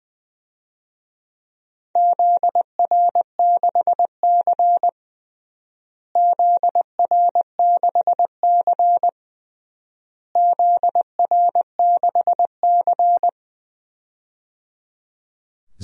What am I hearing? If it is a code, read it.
Morse code, heard as a keyed tone, about 20 wpm, 700 Hz. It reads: ZR6C ZR6C ZR6C
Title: DXCC-100 Call Signs - 4 Characters - Repeated 3x - 20wpm